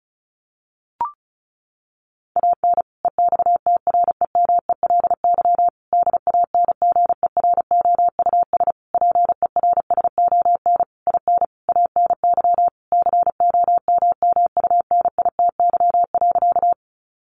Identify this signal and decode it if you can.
Morse code, heard as a keyed tone, about 34 wpm, 700 Hz. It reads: AN EXTREMELY DANGEROUS PERSON IN ANY COMMUNITY.
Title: Sentences from Top 1200 Words 35wpm